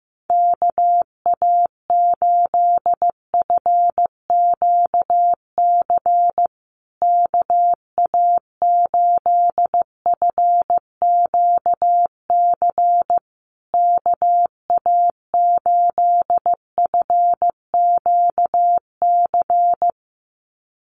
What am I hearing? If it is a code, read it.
Morse code, heard as a keyed tone, about 15 wpm, 700 Hz. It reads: KA8FQC KA8FQC KA8FQC
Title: USA Call Signs Repeated 3x 15wpm